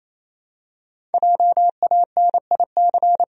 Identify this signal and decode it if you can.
Morse code, heard as a keyed tone, about 28 wpm, 700 Hz. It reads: JANIC
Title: USA Common Names - 28wpm